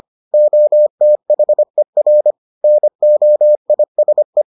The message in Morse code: OTHER NOISE

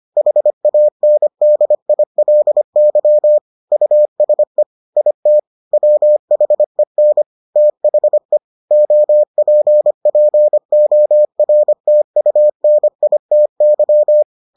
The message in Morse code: HANDILY USE IT WHEN THE OPPORTUNITY

